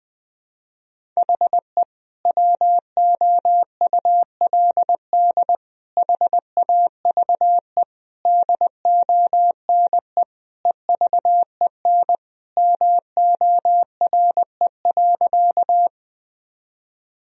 Morse code, heard as a keyed tone, about 21 wpm, 700 Hz. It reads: HE WOULD HAVE DONE EVEN MORE.
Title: Sentences from Top 1000 Words - Review Part 1 - 20wpm